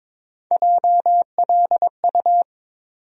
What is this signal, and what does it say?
Morse code, heard as a keyed tone, about 22 wpm, 700 Hz. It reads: JLU
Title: USA Call Sign Postfixes - 3 Letters 22wpm